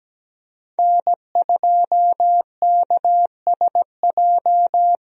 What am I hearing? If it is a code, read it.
Morse code, heard as a keyed tone, about 17 wpm, 700 Hz. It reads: N2KSJ
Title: USA Call Signs - 5 Characters - Repeated 3x - 17wpm